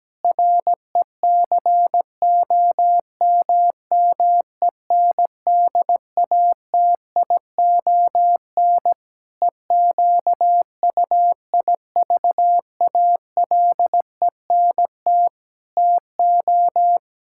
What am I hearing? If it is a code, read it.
Morse code, heard as a keyed tone, about 17 wpm, 700 Hz. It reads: RECOMMENDATION EQUIVALENT TO
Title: Sentences from Top 1600 Words 17wpm